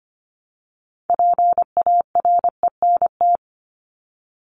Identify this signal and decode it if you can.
Morse code, heard as a keyed tone, about 25 wpm, 700 Hz. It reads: PARENT
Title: ES:S Suffix - Top 100 Words with Base Words 25wpm